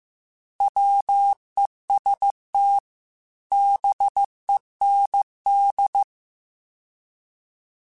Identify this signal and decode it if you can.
Morse code, heard as a keyed tone, about 15 wpm, 800 Hz. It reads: WEST BEND